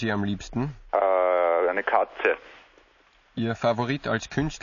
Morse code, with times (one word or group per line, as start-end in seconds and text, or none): none